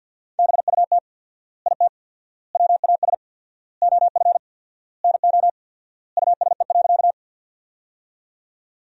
Morse code, heard as a keyed tone, about 52 wpm, 700 Hz. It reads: BUT IT WAS OF NO USE.
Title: 4x Word-Spacing - Sentences from Top 100 Words - 50wpm